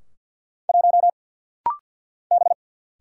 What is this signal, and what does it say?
Morse code, heard as a keyed tone, about 49 wpm, 700 Hz. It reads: 1 B
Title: Single Letter-Number - Rapid-Fire 50wpm